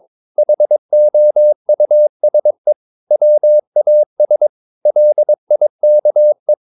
Morse code, heard as a keyed tone, about 22 wpm, 600 Hz. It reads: HOUSE WAS LIKE
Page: Romanen Never af Ken Follet på engelsk.